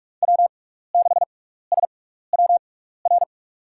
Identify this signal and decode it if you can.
Morse code, heard as a keyed tone, about 46 wpm, spaced as for 15 wpm, 690 Hz. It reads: W6SWR
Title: USA Call Signs - 5 Characters - 6x Character Spacing - 45wpm